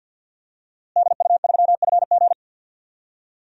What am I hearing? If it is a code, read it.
Morse code, heard as a keyed tone, about 52 wpm, 700 Hz. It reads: DU3FG
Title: DXCC-100 Call Signs - 5 Characters - Repeated 2x - 50wpm